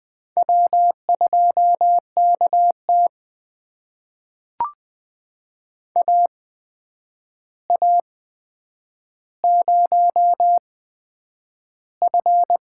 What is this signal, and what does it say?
Morse code, heard as a keyed tone, about 20 wpm, spaced as for 9 wpm, 700 Hz. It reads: W2KT AA0F
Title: USA Call Signs - 4 Characters - 8x Character Spacing - 20wpm